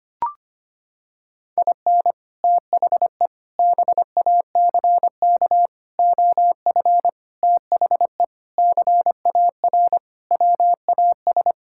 Morse code, heard as a keyed tone, about 25 wpm, 700 Hz. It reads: IN THE BACK OF THE CAR WAS
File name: practice / 25wpm / Sentences with 4-Word Phrases - 25wpm